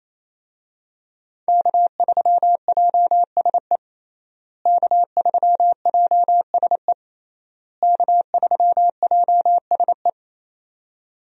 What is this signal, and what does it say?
Morse code, heard as a keyed tone, about 28 wpm, 700 Hz. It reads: K3JSE K3JSE K3JSE
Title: USA Call Signs - 5 Characters - Repeated 3x - 28wpm